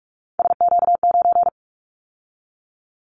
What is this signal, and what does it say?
Morse code, heard as a keyed tone, about 46 wpm, 700 Hz. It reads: SQ9
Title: Callsign Prefixes for Top 25 DXCC - 45wpm